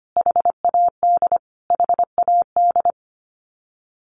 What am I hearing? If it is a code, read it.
Morse code, heard as a keyed tone, about 25 wpm, 700 Hz. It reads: HAD HAD
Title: Top 200 Words Repeated 25wpm